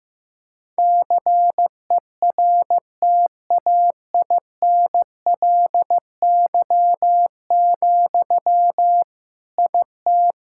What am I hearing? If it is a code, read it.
Morse code, heard as a keyed tone, about 15 wpm, 700 Hz. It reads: CERTAINLY, IT
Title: Sentences from Top 1000 Words - Review Part 6 - 15wpm